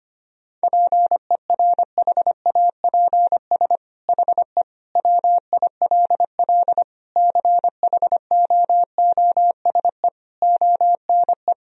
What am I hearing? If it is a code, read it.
Morse code, heard as a keyed tone, about 25 wpm, 700 Hz. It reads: PERHAPS HE WILL CHOOSE ONE
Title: Sentences from Top 1000 Words - Review Part 4 - 25wpm